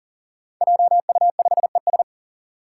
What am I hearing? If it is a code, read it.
Morse code, heard as a keyed tone, about 40 wpm, 700 Hz. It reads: JU5ES
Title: Five Challenging Letter-Number Combinations 40wpm